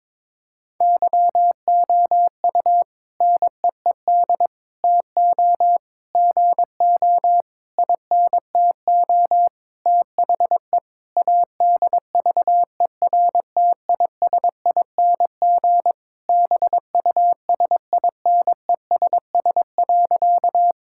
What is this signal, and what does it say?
Morse code, heard as a keyed tone, about 22 wpm, 700 Hz. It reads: YOU NEED TO GO INTO THE ADVERTISING BUSINESS.
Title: Sentences from Top 1000 Words - Review Part 6 - 22wpm